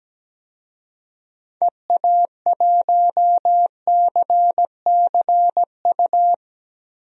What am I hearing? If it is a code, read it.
Morse code, heard as a keyed tone, about 17 wpm, 700 Hz. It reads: EA1CCU
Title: DXCC-100 Call Signs - 6 Characters - Repeated 3x - 17wpm